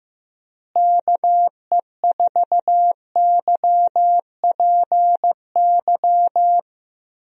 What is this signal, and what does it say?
Morse code, heard as a keyed tone, about 15 wpm, 700 Hz. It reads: KE4YPY
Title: USA Call Signs - 6 Characters - Repeated 3x - 15wpm